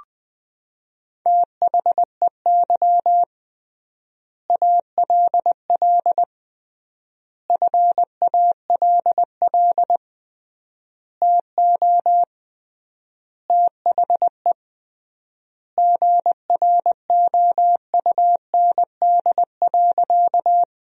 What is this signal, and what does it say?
Morse code, heard as a keyed tone, about 21 wpm, 700 Hz. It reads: THEY ALL FALL TO THE GROUND.
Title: 3x Word-Spacing - Sentences from Top 500 Words - 20wpm